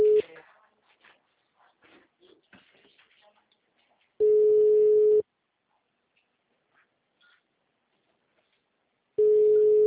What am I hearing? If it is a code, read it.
Ringback tone (Europe, 425 Hz).